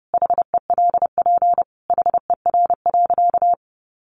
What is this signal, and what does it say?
Morse code, heard as a keyed tone, about 30 wpm, 700 Hz. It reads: HELP HER.
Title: Sentences from Top 1000 Words 30wpm